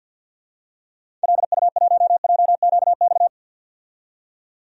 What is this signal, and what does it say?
Morse code, heard as a keyed tone, about 50 wpm, 700 Hz. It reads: LU1JQX
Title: DXCC-100 Call Signs - 6 Characters - 50wpm